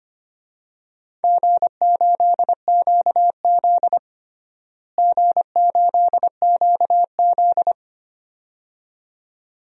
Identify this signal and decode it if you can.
Morse code, heard as a keyed tone, about 25 wpm, 700 Hz. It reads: G8QZ G8QZ
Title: DXCC-100 Call Signs - 4 Characters - Repeated 2x - 25wpm